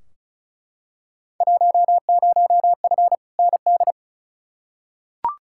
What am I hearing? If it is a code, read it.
Morse code, heard as a keyed tone, about 36 wpm, 690 Hz. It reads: 10F ND
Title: Field Day Exchanges Encore - 35wpm